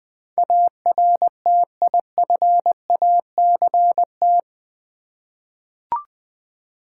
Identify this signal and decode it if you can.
Morse code, heard as a keyed tone, about 20 wpm, 690 Hz. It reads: ARTIFACT
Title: Easy Words with Hints - 20wpm